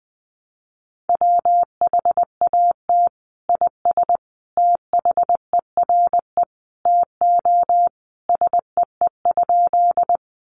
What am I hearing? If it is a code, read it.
Morse code, heard as a keyed tone, about 20 wpm, 700 Hz. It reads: WHAT IS THERE TO SEE?
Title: Sentences from Top 100 Words 20wpm